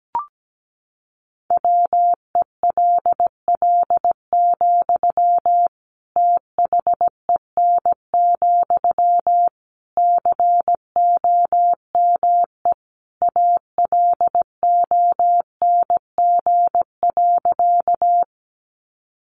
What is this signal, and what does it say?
Morse code, heard as a keyed tone, about 17 wpm, 700 Hz. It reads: WELL, THEN, COME ALONG.
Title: Sentences from Top 700 Words 17wpm